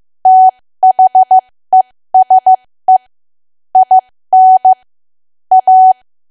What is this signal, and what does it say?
Morse code, heard as a keyed tone, about 15 wpm, spaced as for 13 wpm, 750 Hz. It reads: THESE IN A
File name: cwa / W1AW-Level2 / 13WPM